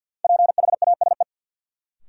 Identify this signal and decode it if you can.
Morse code, heard as a keyed tone, about 53 wpm, 700 Hz. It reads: PHASE
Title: Sentences from Top 1200 Words - New Word Drill - 50wpm